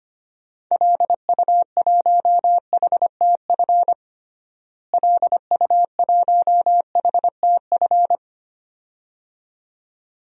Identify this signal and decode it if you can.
Morse code, heard as a keyed tone, about 25 wpm, 700 Hz. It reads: LU1HTF LU1HTF
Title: DXCC-100 Call Signs - 6 Characters - Repeated 2x - 25wpm